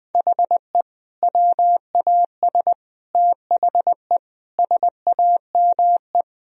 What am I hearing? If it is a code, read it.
Morse code, heard as a keyed tone, about 21 wpm, 700 Hz. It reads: HE WAS THE SAME
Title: Sentences from Top 1000 Words - Review Part 2 - 20wpm